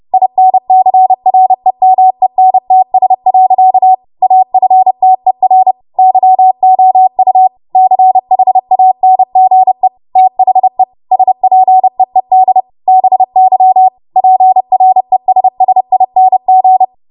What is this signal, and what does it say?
Morse code, heard as a keyed tone, about 30 wpm, 750 Hz. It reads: INCREMENTS. AFTER YOU CHANGE THE SPEED BY PRESSING
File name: Morse / 30WPM